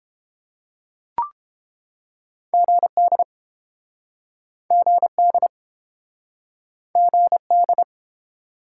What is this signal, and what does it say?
Morse code, heard as a keyed tone, about 29 wpm, 700 Hz. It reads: GD GD GD